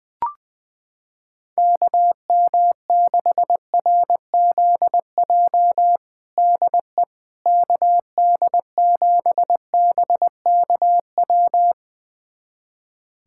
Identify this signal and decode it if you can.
Morse code, heard as a keyed tone, about 20 wpm, 700 Hz. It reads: KM6RZJ DE KD7BKW
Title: USA Callsign DE Callsign - 6 Characters - 20wpm